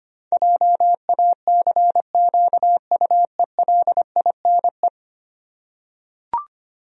Morse code, heard as a keyed tone, about 25 wpm, 700 Hz. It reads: JACQUELINE
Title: USA Common Names - 25wpm